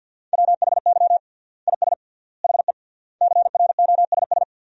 Morse code, heard as a keyed tone, about 50 wpm, 700 Hz. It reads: WHY IS HE CROSS
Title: 3x Word-Spacing - Sentences from Top 300 Words - 50wpm